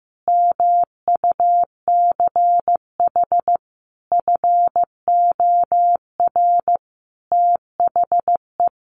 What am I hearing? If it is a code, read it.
Morse code, heard as a keyed tone, about 15 wpm, 700 Hz. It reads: MUCH FOR THE
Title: Sentences from Top 1000 Words 15wpm